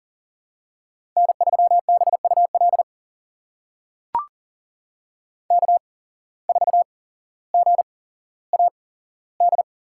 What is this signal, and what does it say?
Morse code, heard as a keyed tone, about 40 wpm, spaced as for 17 wpm, 700 Hz. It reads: N3BUL K4GAD
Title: USA Call Signs - 5 Characters - 8x Character Spacing - 40wpm